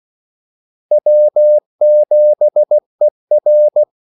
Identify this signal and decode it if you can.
Morse code, heard as a keyed tone, about 16 wpm, 600 Hz. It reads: W7ER
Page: Kaldesignaler 16 wpm